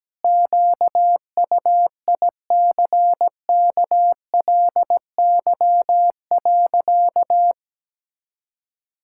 Morse code, heard as a keyed tone, about 17 wpm, 700 Hz. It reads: QUICKLY.
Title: Pangram Challenge v2 - 17wpm